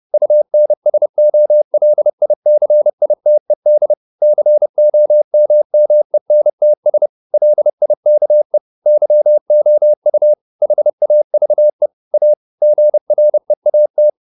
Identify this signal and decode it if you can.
Morse code, heard as a keyed tone, about 30 wpm, 600 Hz. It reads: UNSOLICITED COMMENTS LIKE YOU HAVE A GREAT